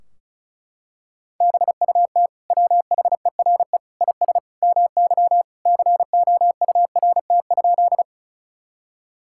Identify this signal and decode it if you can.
Morse code, heard as a keyed tone, about 36 wpm, 700 Hz. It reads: BUT WHERE IS MY COURT?
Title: Sentences from Top 1600 Words 35wpm